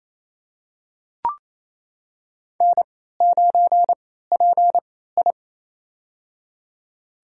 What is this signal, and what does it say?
Morse code, heard as a keyed tone, about 28 wpm, spaced as for 16 wpm, 700 Hz. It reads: N9PI
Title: USA Call Signs - 4 Characters - 3x Character Spacing - 28wpm